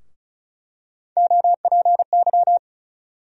Morse code, heard as a keyed tone, about 36 wpm, 700 Hz. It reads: OPY
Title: USA Call Sign Postfixes - 3 Letters 35wpm